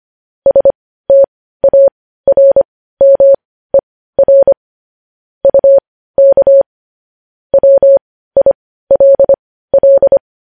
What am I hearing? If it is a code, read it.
Morse code, heard as a keyed tone, about 25 wpm, spaced as for 15 wpm, 550 Hz. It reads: STARMER UK WILL
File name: podcast / News headlines-1x-25x15